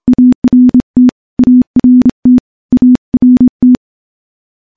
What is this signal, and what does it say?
Morse code, heard as a keyed tone, about 26 wpm, 250 Hz. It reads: ART ART ART
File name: morse-training-25WPM